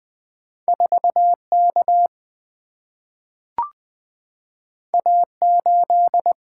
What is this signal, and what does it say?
Morse code, heard as a keyed tone, about 20 wpm, 690 Hz. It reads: 4K A8
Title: Two Letter-Number Combinations - 20wpm